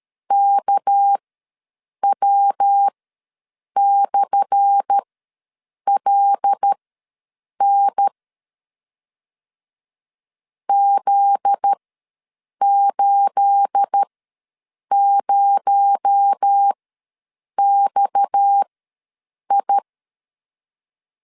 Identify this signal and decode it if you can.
Morse code, heard as a keyed tone, about 13 wpm, spaced as for 7 wpm, 800 Hz. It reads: KW/LN Z80XI
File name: LEZIONI / 10 SIGNS / 10A SLASH ws 12wpm 3443